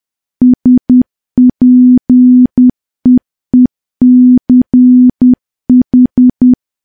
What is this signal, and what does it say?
Morse code, heard as a keyed tone, about 10 wpm, 250 Hz. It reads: SPEECH
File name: morse / morse-training-10WPM